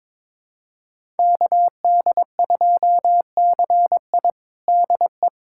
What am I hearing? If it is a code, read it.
Morse code, heard as a keyed tone, about 22 wpm, 700 Hz. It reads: KD2CI DE
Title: USA Callsign DE Callsign - 5 Characters - 22wpm